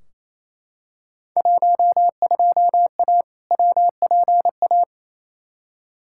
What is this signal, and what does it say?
Morse code, heard as a keyed tone, about 28 wpm, 700 Hz. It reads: 12A WPA
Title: Field Day Exchanges Encore - 28wpm